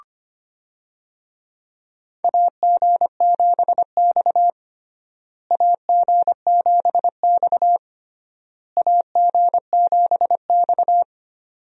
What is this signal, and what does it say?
Morse code, heard as a keyed tone, about 25 wpm, 700 Hz. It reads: AG7X AG7X AG7X
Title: USA Call Signs - 4 Characters - Repeated 3x - 25wpm